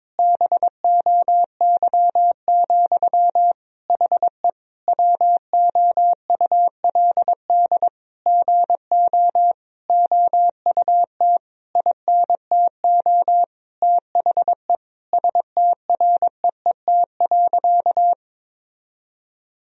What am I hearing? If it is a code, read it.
Morse code, heard as a keyed tone, about 22 wpm, 700 Hz. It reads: BOY, HE WOULD GO OUT INTO THE STREET.
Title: Sentences from Top 1000 Words - Review Part 3 - 22wpm